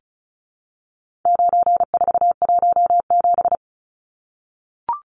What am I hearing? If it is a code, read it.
Morse code, heard as a keyed tone, about 34 wpm, 700 Hz. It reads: 9417
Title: Numbers - 4-Digit 35wpm